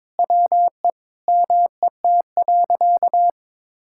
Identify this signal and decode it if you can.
Morse code, heard as a keyed tone, about 22 wpm, 700 Hz. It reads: WE MET.
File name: Sentences with 7-Word Phrases - 22wpm